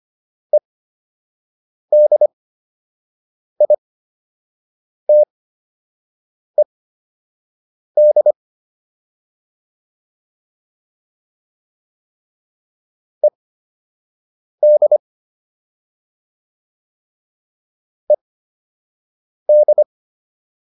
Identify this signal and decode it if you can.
Morse code, heard as a keyed tone, about 25 wpm, spaced as for 6 wpm, 600 Hz. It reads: EDITED ED ED